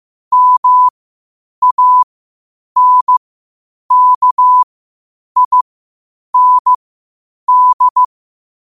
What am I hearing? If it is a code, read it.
Morse code, heard as a keyed tone, about 15 wpm, spaced as for 8 wpm, 1000 Hz. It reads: MANKIND